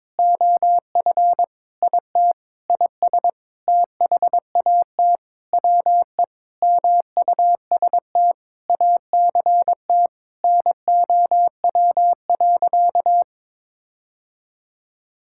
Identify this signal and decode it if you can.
Morse code, heard as a keyed tone, about 22 wpm, 700 Hz. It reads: OF IT IS THAT WE MUST ACT NOW.
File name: Sentences with 9-Word Phrases - 22wpm